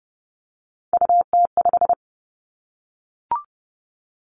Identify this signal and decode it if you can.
Morse code, heard as a keyed tone, about 30 wpm, 700 Hz. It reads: UT5
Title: Callsign Prefixes for Top 25 DXCC - 30wpm